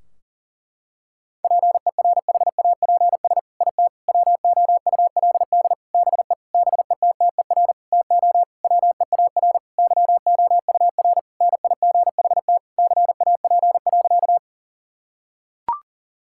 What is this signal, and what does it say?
Morse code, heard as a keyed tone, about 40 wpm, 700 Hz. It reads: PERHAPS IT WOULD BE BETTER TO WEAR YOUR NIGHT CAP.
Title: Sentences from Top 1600 Words 40wpm